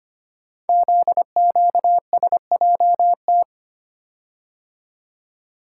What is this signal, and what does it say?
Morse code, heard as a keyed tone, about 25 wpm, 700 Hz. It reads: ZQSJT
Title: Code Groups - 5 Letters - 25wpm